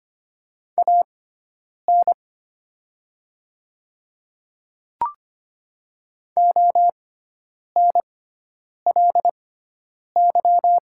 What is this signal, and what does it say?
Morse code, heard as a keyed tone, about 25 wpm, spaced as for 9 wpm, 690 Hz. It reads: AN ONLY
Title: 6x Character Spacing - Top 300 Words - 25wpm